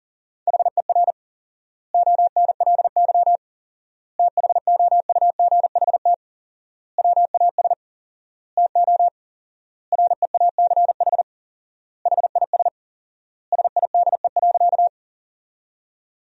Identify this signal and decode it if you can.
Morse code, heard as a keyed tone, about 40 wpm, 700 Hz. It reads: HER ONLY THOUGHT WAS TO REACH HIS SIDE.
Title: 4x Word-Spacing - Sentences from Top 400 Words - 40wpm